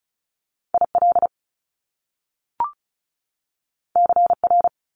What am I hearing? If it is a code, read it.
Morse code, heard as a keyed tone, about 36 wpm, 700 Hz. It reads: IL CR
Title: Top Two Letter Combinations in Words 35wpm